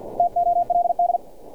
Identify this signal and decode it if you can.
Morse code, heard as a keyed tone, about 50 wpm, 700 Hz. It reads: TOLD
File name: archived / CRSnet